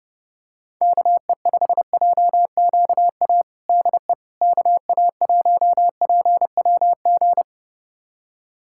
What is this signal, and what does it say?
Morse code, heard as a keyed tone, about 30 wpm, 700 Hz. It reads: KE5JQA DE KA1PWG
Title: USA Callsign DE Callsign 30wpm